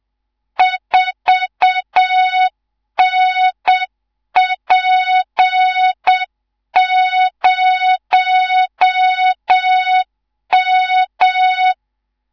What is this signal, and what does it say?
Morse code, heard as a keyed tone, about 7 wpm, 3700 Hz, with noise. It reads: 4NP0M
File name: Morze-HA2MN / hang / 19 GYAKORLAT